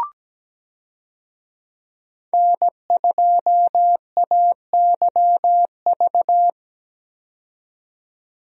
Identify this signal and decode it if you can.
Morse code, heard as a keyed tone, about 17 wpm, 700 Hz. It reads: N2AYV
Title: USA Call Signs - 5 Characters - 17wpm